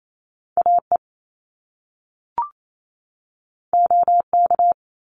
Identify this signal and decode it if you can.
Morse code, heard as a keyed tone, about 28 wpm, 700 Hz. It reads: AE OK
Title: Top Two Letter Combinations in Words 28wpm